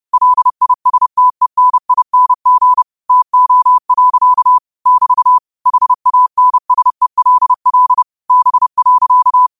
Morse code, heard as a keyed tone, about 30 wpm, 1000 Hz. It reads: LISTENING TO. <BT> HANSELL B.
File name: qotdfiles / 30wpm